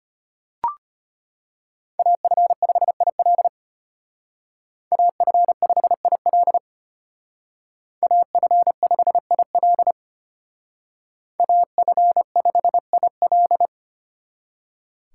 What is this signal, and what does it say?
Morse code, heard as a keyed tone, about 32 wpm, 700 Hz. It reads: AF5IL AF5IL AF5IL AF5IL